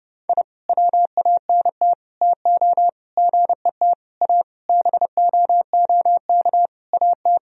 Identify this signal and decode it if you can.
Morse code, heard as a keyed tone, about 30 wpm, 700 Hz. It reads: I WANT TO GET A BOOK AT
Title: Sentences from Top 1000 Words - Review Part 1 - 30wpm